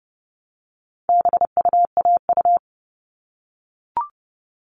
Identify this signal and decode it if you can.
Morse code, heard as a keyed tone, about 30 wpm, 700 Hz. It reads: BUAU
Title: All Cut Numbers - 4-Digit 30wpm